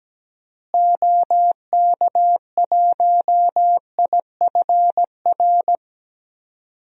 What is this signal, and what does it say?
Morse code, heard as a keyed tone, about 17 wpm, 700 Hz. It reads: OK1IFR
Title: DXCC-100 Call Signs - 6 Characters - Repeated 3x - 17wpm